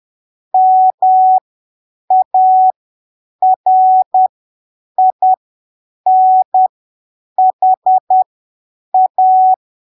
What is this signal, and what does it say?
Morse code, heard as a keyed tone, about 10 wpm, spaced as for 7 wpm, 750 Hz. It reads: MARINHA
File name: arquivos / texto / texto 14 10ppm